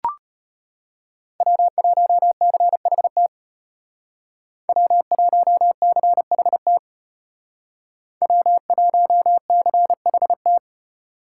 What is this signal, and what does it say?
Morse code, heard as a keyed tone, about 33 wpm, 700 Hz. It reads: W1CHT W1CHT W1CHT